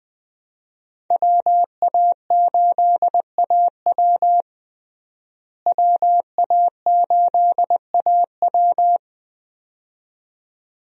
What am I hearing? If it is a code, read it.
Morse code, heard as a keyed tone, about 20 wpm, 700 Hz. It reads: WA8AW WA8AW
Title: USA Call Signs - 5 Characters - Repeated 2x - 20wpm